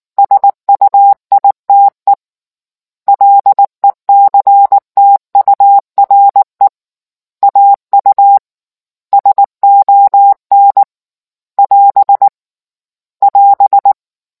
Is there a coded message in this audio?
Yes: Morse code from 0.2 to 13.9 s, heard as a keyed tone, about 19 wpm, 800 Hz. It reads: SUITE LECTURE AU SON <AS> <AS>